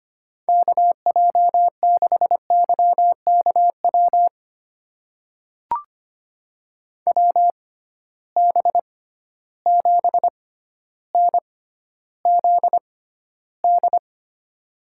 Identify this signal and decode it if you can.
Morse code, heard as a keyed tone, about 25 wpm, spaced as for 13 wpm, 700 Hz. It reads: KJ6YKW WB7NZD
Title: USA Call Signs - 6 Characters - 6x Character Spacing - 25wpm